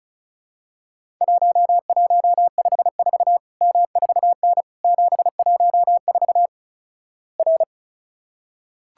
Morse code, heard as a keyed tone, about 36 wpm, 700 Hz. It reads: 1154 M4N 714 R
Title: Worked All Europe DX Contest - QTC Exchange - 35wpm